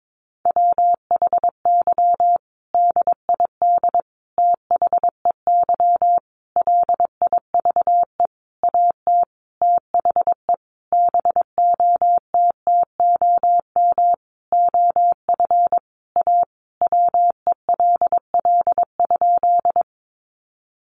Morse code, heard as a keyed tone, about 22 wpm, 700 Hz. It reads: WHY DID THEY LIVE AT THE BOTTOM OF A WELL?
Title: Sentences from Top 900 Words 22wpm